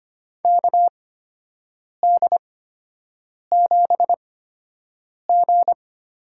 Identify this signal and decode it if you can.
Morse code, heard as a keyed tone, about 25 wpm, spaced as for 7 wpm, 700 Hz. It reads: KD7G
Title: USA Call Signs - 6 Characters - 8x Character Spacing - 25wpm